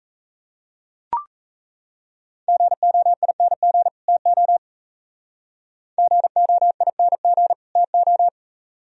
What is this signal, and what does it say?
Morse code, heard as a keyed tone, about 33 wpm, 700 Hz. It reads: GOING TO GOING TO
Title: Speed-Racing - Sets of 2 Words 28wpm